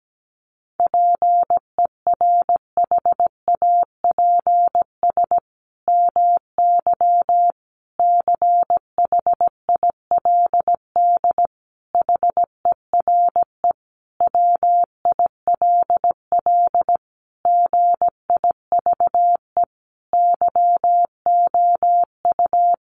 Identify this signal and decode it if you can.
Morse code, heard as a keyed tone, about 17 wpm, 700 Hz. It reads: PERHAPS MY CHILD HERE WILL GIVE YOU
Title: Sentences from Top 1100 Words 17wpm